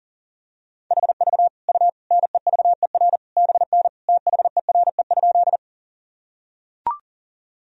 Morse code, heard as a keyed tone, about 40 wpm, 700 Hz. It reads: HV U NEVER BN THERE?
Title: Sentences from Top 300 Words and Abbreviations 40wpm